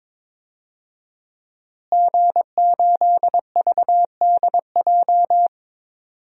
Morse code, heard as a keyed tone, about 21 wpm, 700 Hz. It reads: G8VDJ
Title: DXCC-100 Call Signs - 5 Characters - Repeated 2x - 22wpm